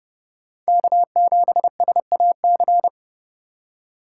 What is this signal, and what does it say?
Morse code, heard as a keyed tone, about 30 wpm, 700 Hz. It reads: K7SAC
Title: Speed Racing - USA 1x3 Call Signs 30wpm